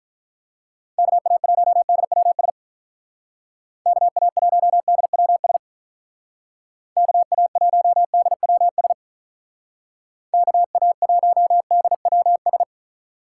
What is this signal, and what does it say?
Morse code, heard as a keyed tone, about 43 wpm, 700 Hz. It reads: KA1DWS KA1DWS KA1DWS KA1DWS
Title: Speed Racing - USA Call Signs 35wpm